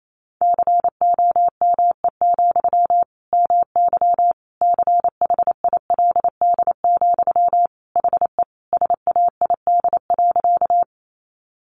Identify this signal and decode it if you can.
Morse code, heard as a keyed tone, about 28 wpm, 700 Hz. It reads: COME, MY CHILD, HE SAID.
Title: Sentences from Top 800 Words 28wpm